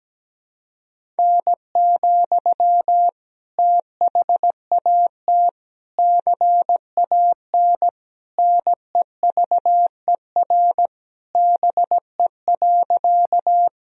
Morse code, heard as a keyed tone, about 17 wpm, 700 Hz. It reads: N, THAT CAN NEVER BE.
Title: Sentences from Top 300 Words and Abbreviations 17wpm